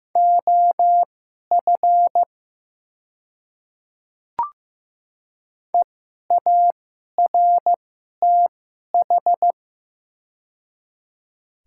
Morse code, heard as a keyed tone, about 15 wpm, spaced as for 11 wpm, 700 Hz. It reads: OF EARTH
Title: 2x Character Spacing - Top 500 Words - 15wpm